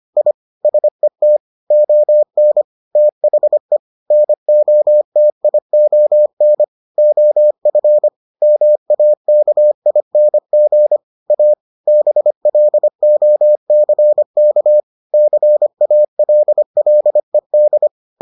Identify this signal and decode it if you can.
Morse code, heard as a keyed tone, about 25 wpm, 600 Hz. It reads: I SET ON THE NOTION OF MAKING A BLOCK CALLED